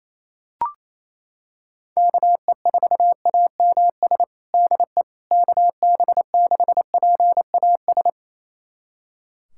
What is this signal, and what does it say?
Morse code, heard as a keyed tone, about 28 wpm, 700 Hz. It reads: KE4AMS DE KB6PAS
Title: USA Callsign DE Callsign 28wpm